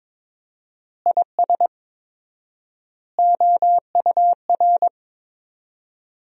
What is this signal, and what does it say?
Morse code, heard as a keyed tone, about 22 wpm, 700 Hz. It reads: IS OUR
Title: 4x Word-Spacing - Sentences from Top 500 Words - 22wpm